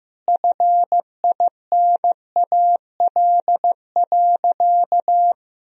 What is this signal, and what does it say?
Morse code, heard as a keyed tone, about 15 wpm, 700 Hz. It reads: FINAL.
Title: Sentences from Top 1000 Words - Review Part 2 - 15wpm